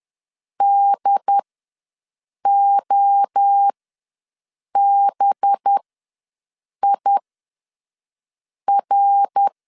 Morse code, heard as a keyed tone, about 10 wpm, spaced as for 5 wpm, 800 Hz. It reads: DOBIR
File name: LEZIONI / 7 GF9Z / 7 GF9Z ws 10wpm 3919